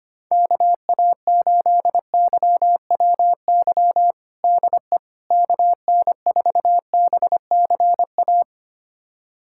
Morse code, heard as a keyed tone, about 25 wpm, 700 Hz. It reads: KA8YWY DE KN4BCA
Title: USA Callsign DE Callsign - 6 Characters - 25wpm